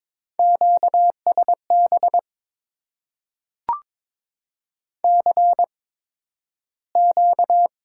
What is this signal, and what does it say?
Morse code, heard as a keyed tone, about 22 wpm, spaced as for 12 wpm, 700 Hz. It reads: QSB CQ
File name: Top 90 QSO Elements - 8x Character Spacing - 22wpm